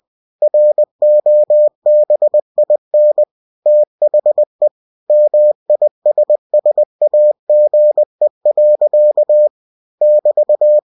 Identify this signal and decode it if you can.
Morse code, heard as a keyed tone, about 20 wpm, 600 Hz. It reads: ROBIN THE MISSAGE. <BT>
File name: cw-books / winnie-the-pooh / winnie-20wpm-page-0098